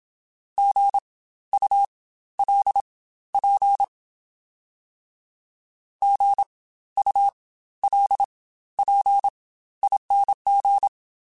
Morse code, heard as a keyed tone, about 27 wpm, spaced as for 14 wpm, 800 Hz. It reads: GULP GULPING
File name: Spaced Alphas / Spaced Alphas mp3s / Gs sp @25